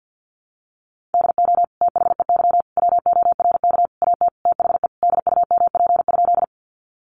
Morse code, heard as a keyed tone, about 48 wpm, 700 Hz. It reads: DO THEY WORK AT THE DUMP?
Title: Sentences from Top 1300 Words 50wpm